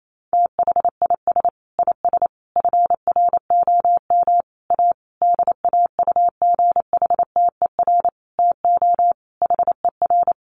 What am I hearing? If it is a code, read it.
Morse code, heard as a keyed tone, about 28 wpm, 700 Hz. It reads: THIS IS FROM A DAUGHTER TO HER
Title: Sentences from Top 1500 Words 28wpm